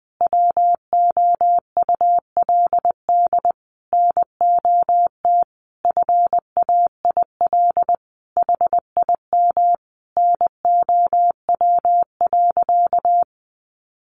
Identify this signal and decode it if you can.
Morse code, heard as a keyed tone, about 20 wpm, 700 Hz. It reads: WOULD NOT FAIL HIM NOW.
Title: Sentences from Top 1100 Words 20wpm